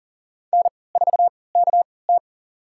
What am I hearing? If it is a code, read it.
Morse code, heard as a keyed tone, about 40 wpm, spaced as for 23 wpm, 700 Hz. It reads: N4KT
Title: USA Call Signs - 3x Character Spacing - 40wpm